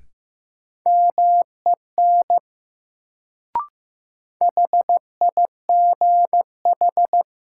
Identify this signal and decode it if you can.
Morse code, heard as a keyed tone, about 15 wpm, 700 Hz. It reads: MEN HIGH
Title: Sentences from Top 200 Words - New Word Drill - 15wpm